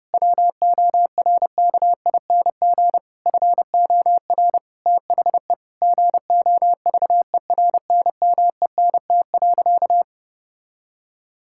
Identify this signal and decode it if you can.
Morse code, heard as a keyed tone, about 30 wpm, 700 Hz. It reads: WORKING FOR THE GOVERNMENT.
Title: Sentences from Top 1000 Words - Review Part 6 - 30wpm